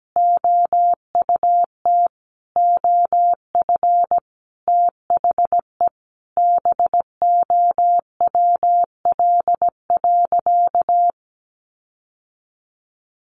Morse code, heard as a keyed tone, about 17 wpm, 700 Hz. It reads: OUT OF THE BOWL.
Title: Sentences from Top 1200 Words 17wpm